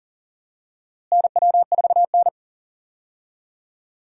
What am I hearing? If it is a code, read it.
Morse code, heard as a keyed tone, about 40 wpm, 700 Hz. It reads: NW4N
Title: USA Call Signs - 4 Characters - Repeated 2x - 40wpm